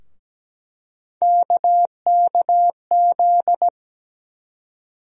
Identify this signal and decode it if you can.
Morse code, heard as a keyed tone, about 17 wpm, 700 Hz. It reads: KKZ
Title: USA Call Sign Postfixes - 3 Letters 17wpm